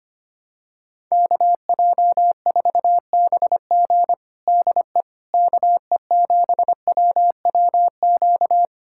Morse code, heard as a keyed tone, about 25 wpm, 700 Hz. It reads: KJ4BG DE KE7WWQ
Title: USA Callsign DE Callsign 25wpm